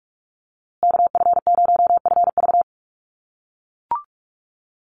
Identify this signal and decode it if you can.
Morse code, heard as a keyed tone, about 44 wpm, 700 Hz. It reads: KF0FV
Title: USA Call Signs Repeated 3x 45wpm